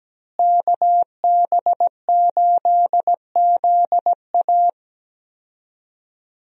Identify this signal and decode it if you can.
Morse code, heard as a keyed tone, about 17 wpm, 700 Hz. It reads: KB8ZA
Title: USA Call Signs - 5 Characters - Repeated 3x - 17wpm